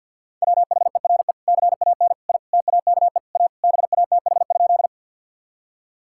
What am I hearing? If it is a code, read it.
Morse code, heard as a keyed tone, about 50 wpm, 700 Hz. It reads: WHERE CAN I TAKE A BATH?
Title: Sentences from Top 1600 Words 50wpm